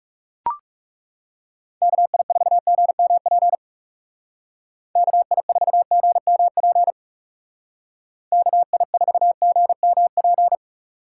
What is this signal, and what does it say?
Morse code, heard as a keyed tone, about 35 wpm, 690 Hz. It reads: KI4GMP KI4GMP KI4GMP
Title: Speed Racing - USA Call Signs 30wpm